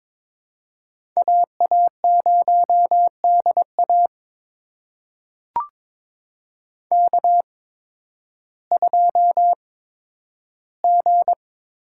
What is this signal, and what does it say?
Morse code, heard as a keyed tone, about 22 wpm, spaced as for 12 wpm, 700 Hz. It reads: AA0DA K2G
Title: USA Call Signs - 5 Characters - 8x Character Spacing - 22wpm